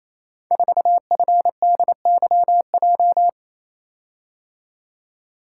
Morse code, heard as a keyed tone, about 28 wpm, 700 Hz. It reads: 4FDYJ
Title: Code Groups - 5 Letter:Numbers - 28wpm